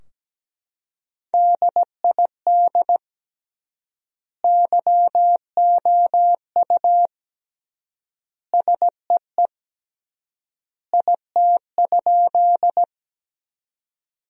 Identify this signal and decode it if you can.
Morse code, heard as a keyed tone, about 17 wpm, 700 Hz. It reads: DID YOU SEE IT?
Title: 3x Word-Spacing - Sentences from Top 100 Words - 17wpm